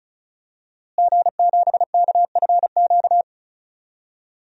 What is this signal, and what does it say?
Morse code, heard as a keyed tone, about 36 wpm, 700 Hz. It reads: G7KFQ
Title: DXCC-100 Call Signs - 5 Characters - Repeated 2x - 35wpm